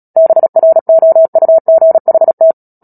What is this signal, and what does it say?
Morse code, heard as a keyed tone, about 37 wpm, 650 Hz. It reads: BROUGHT
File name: Book Two / 10-Chapter 10 - Epilogue - 36 WPM